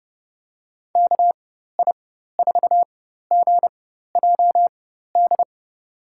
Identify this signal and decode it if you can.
Morse code, heard as a keyed tone, about 30 wpm, spaced as for 14 wpm, 700 Hz. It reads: KI4GJD
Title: USA Call Signs - 6 Characters - 4x Character Spacing - 30wpm